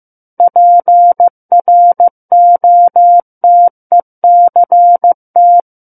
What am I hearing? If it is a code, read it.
Morse code, heard as a keyed tone, about 15 wpm, 700 Hz. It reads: PROTECT
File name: Book One / 06-Chapter 6 - 15 WPM